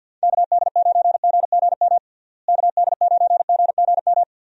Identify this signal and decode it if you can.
Morse code, heard as a keyed tone, about 52 wpm, 700 Hz. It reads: KD9GGM KD9GGM
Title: USA Call Signs - 6 Characters - Repeated 3x - 50wpm